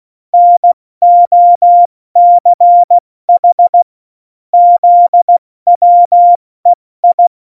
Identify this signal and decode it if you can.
Morse code, heard as a keyed tone, about 16 wpm, 700 Hz. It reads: NOCH ZWEI
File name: Morsekurs / Witze / MP3, Witz 08. Mai 2018, 70 F80